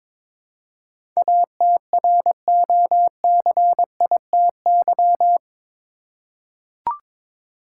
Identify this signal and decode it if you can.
Morse code, heard as a keyed tone, about 22 wpm, 700 Hz. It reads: ATROCITY
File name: practice / 22wpm / Eight Letter Words - 22wpm